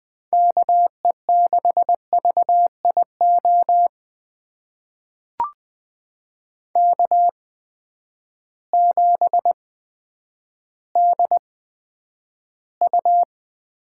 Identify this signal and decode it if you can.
Morse code, heard as a keyed tone, about 20 wpm, spaced as for 10 wpm, 700 Hz. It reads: KE6VIO K7DU
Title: USA Call Signs - 8x Character Spacing - 20wpm